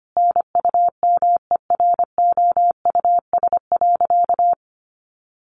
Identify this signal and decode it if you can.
Morse code, heard as a keyed tone, about 25 wpm, 700 Hz. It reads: NUMEROUS.
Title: Sentences from Top 1400 Words 25wpm